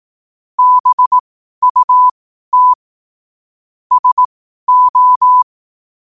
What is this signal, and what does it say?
Morse code, heard as a keyed tone, about 18 wpm, spaced as for 12 wpm, 1000 Hz. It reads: BUT SO
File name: qotdfiles / 12wpm